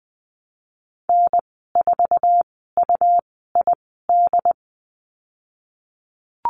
Morse code, heard as a keyed tone, about 20 wpm, spaced as for 15 wpm, 700 Hz. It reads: N4UID
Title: 122 Lesson 20g - Number 3 - Call Signs 20wpm